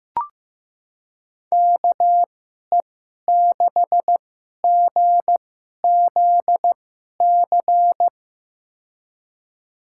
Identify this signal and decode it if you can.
Morse code, heard as a keyed tone, about 15 wpm, spaced as for 11 wpm, 700 Hz. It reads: KE6GZC
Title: USA Call Signs - 6 Characters - 2x Character Spacing - 15wpm